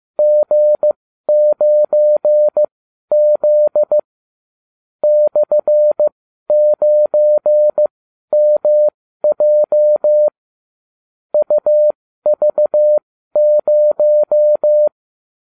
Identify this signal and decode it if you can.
Morse code, heard as a keyed tone, about 15 wpm, spaced as for 12 wpm, 600 Hz. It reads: G9Z /9MJ UV0